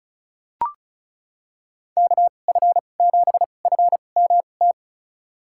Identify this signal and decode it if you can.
Morse code, heard as a keyed tone, about 36 wpm, spaced as for 26 wpm, 690 Hz. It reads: KF7FMT
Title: USA Call Signs - 6 Characters - 2x Character Spacing - 35wpm